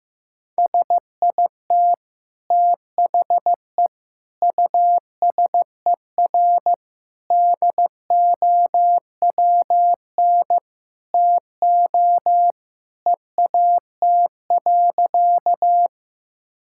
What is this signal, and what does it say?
Morse code, heard as a keyed tone, about 15 wpm, 700 Hz. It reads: SIT THE USER DOWN TO EAT.
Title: Sentences from Top 1000 Words - Review Part 5 - 15wpm